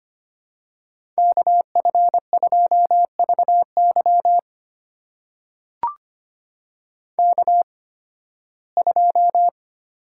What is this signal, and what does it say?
Morse code, heard as a keyed tone, about 25 wpm, spaced as for 16 wpm, 700 Hz. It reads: KF2VY K2
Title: USA Call Signs - 8x Character Spacing - 25wpm